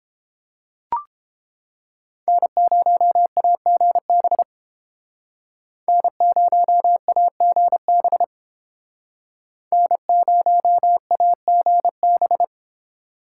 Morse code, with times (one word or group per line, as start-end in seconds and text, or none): 2.3-4.4 N0AGB
5.9-8.2 N0AGB
9.7-12.4 N0AGB